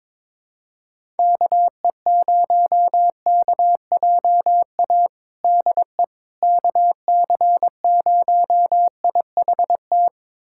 Morse code, heard as a keyed tone, about 22 wpm, 700 Hz. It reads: KE0KJA DE KC0IHT
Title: USA Callsign DE Callsign - 6 Characters - 22wpm